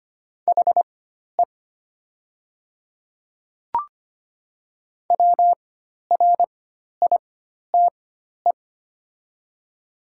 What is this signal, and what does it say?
Morse code, heard as a keyed tone, about 25 wpm, spaced as for 12 wpm, 700 Hz. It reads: HE WRITE
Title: 4x Character Spacing - Top 200 Words - 25wpm